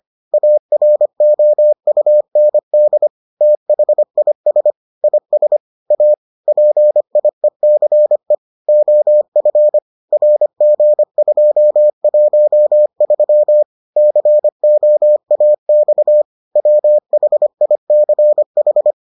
Morse code, heard as a keyed tone, about 25 wpm, 600 Hz. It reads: AROUND THIS IS A PIECE OF RG213 COAX WHICH